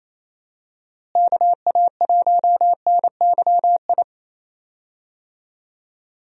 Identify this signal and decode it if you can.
Morse code, heard as a keyed tone, about 28 wpm, 700 Hz. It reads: KA1NYI
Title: USA Call Signs - 6 Characters - Repeated 2x - 28wpm